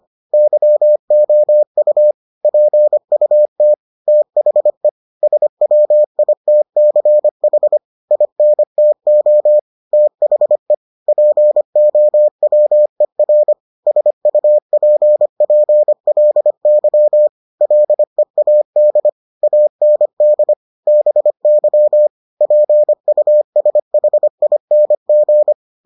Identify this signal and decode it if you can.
Morse code, heard as a keyed tone, about 25 wpm, 600 Hz. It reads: YOU PUT THE SWITCH INTO THE POWER SUPPLY LEAD AND BY PUSHING